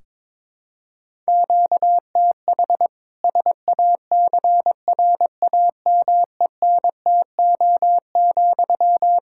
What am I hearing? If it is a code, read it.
Morse code, heard as a keyed tone, about 22 wpm, 700 Hz. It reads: QTH SACRAMENTO,
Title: Practice QSO - Exchange 1 - Beginning - 22wpm